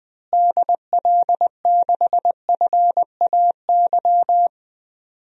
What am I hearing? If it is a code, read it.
Morse code, heard as a keyed tone, about 20 wpm, 700 Hz. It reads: DL6FAY
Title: DXCC-100 Call Signs - 6 Characters - Repeated 3x - 20wpm